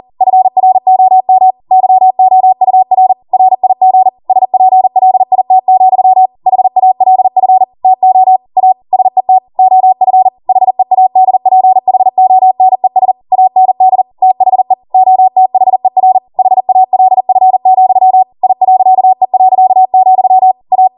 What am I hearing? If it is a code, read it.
Morse code, heard as a keyed tone, about 40 wpm, 750 Hz. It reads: FROM YOUR RIG SPLIT, HALF TO A SET OF HEADPHONES AND THE OTHER HALF, I.E., A